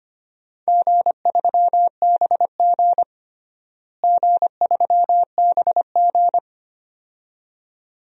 Morse code, heard as a keyed tone, about 25 wpm, 700 Hz. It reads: G3BG G3BG
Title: DXCC-100 Call Signs - 4 Characters - Repeated 2x - 25wpm